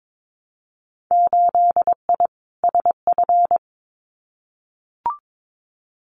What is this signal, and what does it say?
Morse code, heard as a keyed tone, about 22 wpm, 700 Hz. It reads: 8I SF
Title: Winter Field Day Exchanges Encore 22wpm